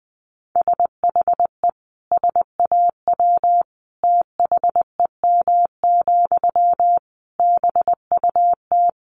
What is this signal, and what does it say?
Morse code, heard as a keyed tone, about 20 wpm, 700 Hz. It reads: SHE SAW THEM, BUT
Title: Sentences from Top 1000 Words 20wpm